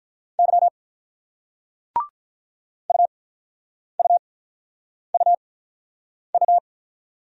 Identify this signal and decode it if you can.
Morse code, heard as a keyed tone, about 46 wpm, spaced as for 9 wpm, 700 Hz. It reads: Y UUUU